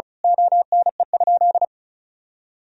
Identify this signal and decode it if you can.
Morse code, heard as a keyed tone, about 34 wpm, 700 Hz. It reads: ONE?
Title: Sentences from Top 1000 Words - Review Part 4 - 35wpm